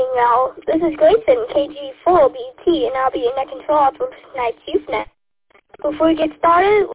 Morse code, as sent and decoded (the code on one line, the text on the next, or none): none